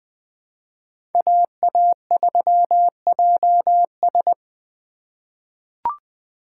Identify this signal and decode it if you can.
Morse code, heard as a keyed tone, about 20 wpm, 700 Hz. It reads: AA3JS
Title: USA Call Signs - 5 Characters - 8x Character Spacing - 20wpm